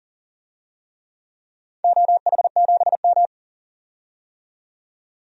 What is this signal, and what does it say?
Morse code, heard as a keyed tone, about 40 wpm, 700 Hz. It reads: OH7M
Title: DXCC-100 Call Signs - 4 Characters - 40wpm